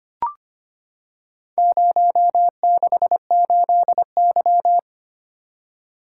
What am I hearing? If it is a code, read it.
Morse code, heard as a keyed tone, about 25 wpm, 700 Hz. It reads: 068Y
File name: practice / 25wpm / Four Letter-Number Combinations - 25wpm